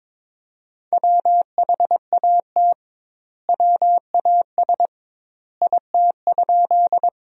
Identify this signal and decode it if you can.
Morse code, heard as a keyed tone, about 21 wpm, 700 Hz. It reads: WHAT WAS IT?
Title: 2x Word-Spacing - Sentences from Top 100 Words - 22wpm